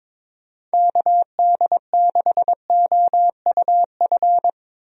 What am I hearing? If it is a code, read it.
Morse code, heard as a keyed tone, about 22 wpm, 700 Hz. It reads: KD6OUF
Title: USA Call Signs - 6 Characters - Repeated 3x - 22wpm